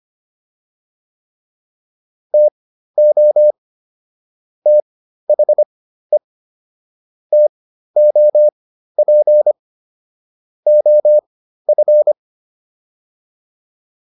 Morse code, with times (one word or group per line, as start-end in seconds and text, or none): 2.3-3.5 TO
4.7-6.2 THE
7.3-9.5 TOP
10.7-12.1 OF